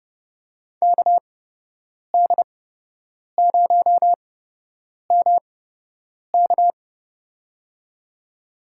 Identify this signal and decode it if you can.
Morse code, heard as a keyed tone, about 32 wpm, spaced as for 8 wpm, 700 Hz. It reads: KD0MK
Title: USA Call Signs - 8x Character Spacing - 30wpm